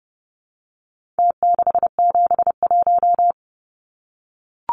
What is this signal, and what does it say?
Morse code, heard as a keyed tone, about 30 wpm, 700 Hz. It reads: T671
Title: Numbers - Contest Style - 4-Digit 30wpm